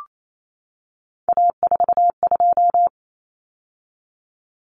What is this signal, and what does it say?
Morse code, heard as a keyed tone, about 28 wpm, 700 Hz. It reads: A42
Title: Common Cut Numbers - 3-Digit 28wpm